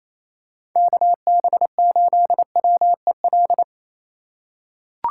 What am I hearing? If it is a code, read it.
Morse code, heard as a keyed tone, about 28 wpm, 700 Hz. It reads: KB8WEL
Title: USA Call Signs - 6 Characters - 3x Character Spacing - 28wpm